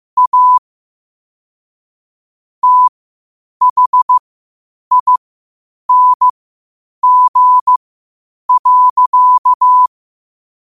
Morse code, heard as a keyed tone, about 15 wpm, spaced as for 8 wpm, 1000 Hz. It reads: A THING.
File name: qotdfiles / 8wpm